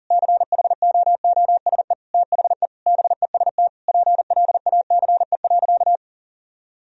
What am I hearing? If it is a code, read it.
Morse code, heard as a keyed tone, about 37 wpm, 700 Hz. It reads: CHOOSE THE BEST PLACE.
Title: Sentences from Top 1000 Words - Review Part 4 - 40wpm